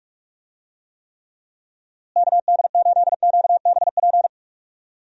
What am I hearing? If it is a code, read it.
Morse code, heard as a keyed tone, about 46 wpm, 700 Hz. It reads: KD8QBP
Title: USA Call Signs - 6 Characters - Repeated 3x - 45wpm